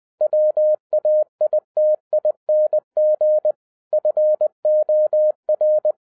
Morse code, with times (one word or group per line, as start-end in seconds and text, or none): 0.2-3.5 WAITING
3.9-5.9 FOR